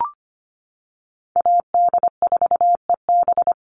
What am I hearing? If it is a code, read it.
Morse code, heard as a keyed tone, about 25 wpm, 700 Hz. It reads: AD4EB
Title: Call Signs - 25wpm